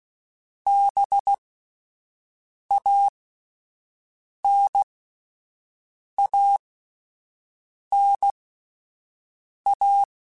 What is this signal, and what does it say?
Morse code, heard as a keyed tone, about 16 wpm, spaced as for 5 wpm, 800 Hz. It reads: BANANA